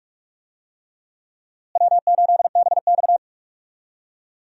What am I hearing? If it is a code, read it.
Morse code, heard as a keyed tone, about 41 wpm, 700 Hz. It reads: W8BX
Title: USA Call Signs - 4 Characters - Repeated 3x - 45wpm